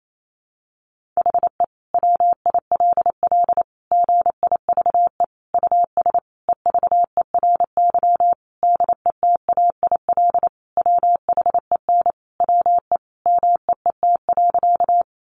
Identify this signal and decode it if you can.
Morse code, heard as a keyed tone, about 28 wpm, 700 Hz. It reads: HE WILL GIVE US EVERY DETAIL WHEN WE MEET.
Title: Sentences from Top 1500 Words 28wpm